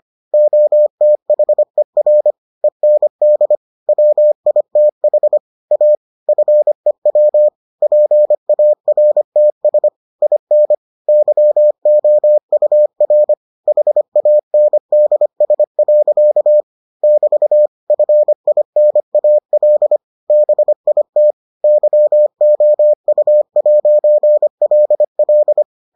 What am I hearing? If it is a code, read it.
Morse code, heard as a keyed tone, about 25 wpm, 600 Hz. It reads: OTHER END WITH A FEW PARTS IN YOUR HANDS. <BT> FINAL BIT YOU'LL